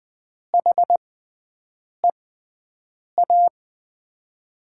Morse code, heard as a keyed tone, about 20 wpm, spaced as for 7 wpm, 700 Hz. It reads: HEA
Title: USA Call Signs - 6x Character Spacing - 20wpm